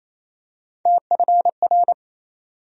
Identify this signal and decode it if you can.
Morse code, heard as a keyed tone, about 28 wpm, 700 Hz. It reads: TFR
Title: USA Call Sign Postfixes - 3 Letters 28wpm